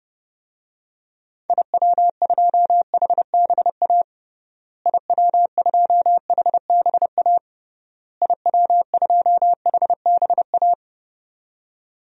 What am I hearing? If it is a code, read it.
Morse code, heard as a keyed tone, about 32 wpm, 700 Hz. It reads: IW2HBA IW2HBA IW2HBA
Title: DXCC-100 Call Signs - 6 Characters - Repeated 3x - 30wpm